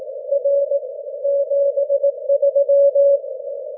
Morse code, heard as a keyed tone, about 19 wpm, 550 Hz. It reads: R 73
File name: wp / vp2v w9dr 6m cw 18 jun 2013 1817z1